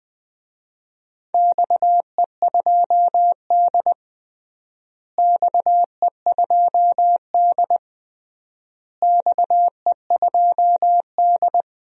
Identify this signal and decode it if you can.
Morse code, heard as a keyed tone, about 20 wpm, 700 Hz. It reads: XE2D XE2D XE2D
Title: DXCC-100 Call Signs - 4 Characters - Repeated 3x - 20wpm